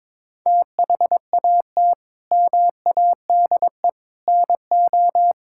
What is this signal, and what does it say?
Morse code, heard as a keyed tone, about 21 wpm, 700 Hz. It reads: THAT MADE NO
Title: Sentences from Top 1000 Words - Review Part 5 - 22wpm